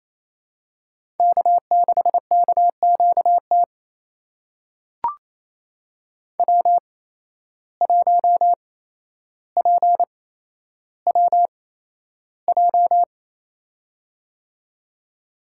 Morse code, heard as a keyed tone, about 28 wpm, spaced as for 12 wpm, 700 Hz. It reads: K6KQT W1PWJ
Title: USA Call Signs - 5 Characters - 8x Character Spacing - 28wpm